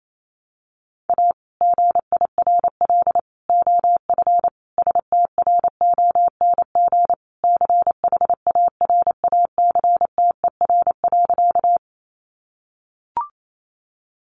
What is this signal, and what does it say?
Morse code, heard as a keyed tone, about 28 wpm, 700 Hz. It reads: A GIRL OF STRONG CHARACTER.
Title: Sentences from Top 1200 Words 28wpm